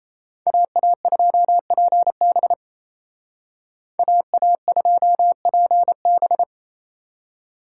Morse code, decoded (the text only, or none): AA2PB AA2PB